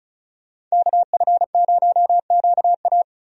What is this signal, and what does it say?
Morse code, heard as a keyed tone, about 36 wpm, 700 Hz. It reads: KF0QA
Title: USA Call Signs - 5 Characters - 35wpm